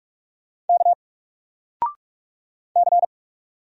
Morse code, heard as a keyed tone, about 44 wpm, 700 Hz. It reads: K C
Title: Single Letter-Number 45wpm